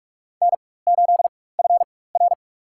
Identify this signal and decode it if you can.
Morse code, heard as a keyed tone, about 46 wpm, spaced as for 21 wpm, 700 Hz. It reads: N8FR
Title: USA Call Signs - 4 Characters - 4x Character Spacing - 45wpm